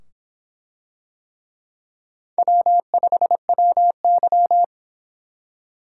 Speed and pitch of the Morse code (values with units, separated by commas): 25 wpm, 700 Hz